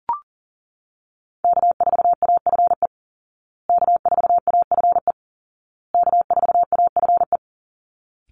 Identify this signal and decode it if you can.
Morse code, heard as a keyed tone, about 40 wpm, 700 Hz. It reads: K4AFE K4AFE K4AFE
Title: Call Signs Repeated 2x - 40wpm